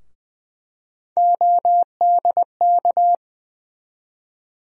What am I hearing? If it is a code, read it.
Morse code, heard as a keyed tone, about 20 wpm, 700 Hz. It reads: ODK
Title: USA Call Sign Postfixes - 3 Letters 20wpm